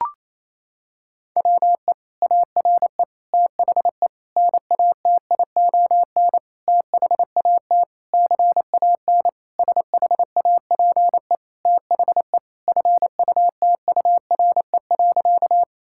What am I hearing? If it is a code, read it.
Morse code, heard as a keyed tone, about 28 wpm, 700 Hz. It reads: WE ARE THE NATION THAT CAN SHAPE THE FUTURE.
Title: Sentences from Top 1000 Words - Review Part 6 - 28wpm